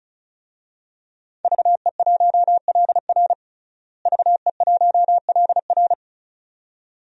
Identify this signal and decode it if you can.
Morse code, heard as a keyed tone, about 34 wpm, 700 Hz. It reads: VE1LR VE1LR
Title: DXCC-100 Call Signs - 5 Characters - Repeated 2x - 35wpm